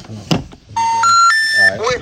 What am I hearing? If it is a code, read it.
Special information tone, with speech.